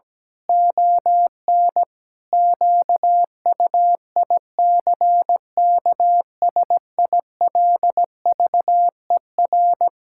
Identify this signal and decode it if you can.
Morse code, heard as a keyed tone, about 17 wpm, 700 Hz. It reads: ON QUICKSILVER
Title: Pangram Challenge v2 - 17wpm